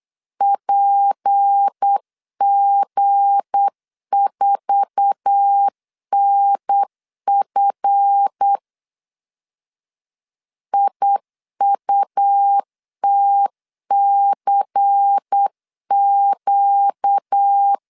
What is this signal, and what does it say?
Morse code, heard as a keyed tone, about 8 wpm, 800 Hz. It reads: PG4NF IUTCQ
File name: CW / DET / 1 9 1500C S B 8wpm 5624